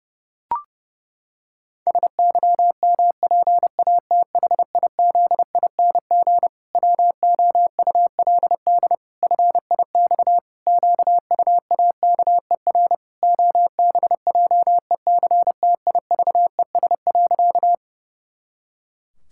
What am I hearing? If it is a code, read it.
Morse code, heard as a keyed tone, about 30 wpm, 700 Hz. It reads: SYMPATHIZING WOULD FIX QUAKER OBJECTIVES.
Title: Pangram Challenge v2 - 30wpm